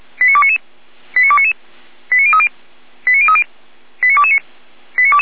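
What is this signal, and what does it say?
Five-tone selective call (ZVEI), heard as quick runs of five tones, tones of about 70 ms. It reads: 89299 89299 89030 89030 89229
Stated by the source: ZVEI five-tone selective call received by SDR, from the Feuerwehr Orlamünde audio archive.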